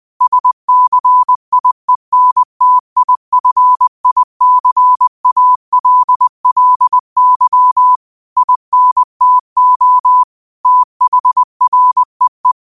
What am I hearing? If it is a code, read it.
Morse code, heard as a keyed tone, about 20 wpm, 1000 Hz. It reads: SCIENTIFICALLY INTO THREE
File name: qotdfiles / 20wpm